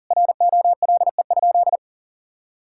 Morse code, heard as a keyed tone, about 40 wpm, 700 Hz. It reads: ROLE?
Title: Sentences from Top 1000 Words - Review Part 6 - 40wpm